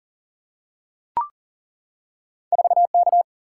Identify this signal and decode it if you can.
Morse code, heard as a keyed tone, about 38 wpm, 700 Hz. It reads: 4K
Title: Single-Letter Single-Number 40wpm